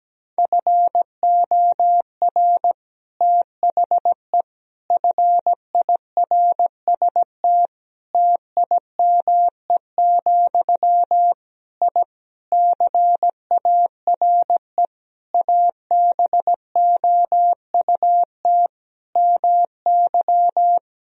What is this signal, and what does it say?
Morse code, heard as a keyed tone, about 17 wpm, 700 Hz. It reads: FOR THE FIRST TIME, I CARE ABOUT MY
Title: Sentences from Top 1000 Words - Review Part 2 - 17wpm